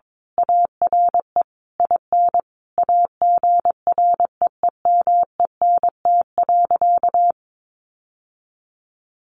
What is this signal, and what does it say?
Morse code, heard as a keyed tone, about 22 wpm, 700 Hz. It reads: ARE IN AGREEMENT.
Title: Sentences from Top 1500 Words 22wpm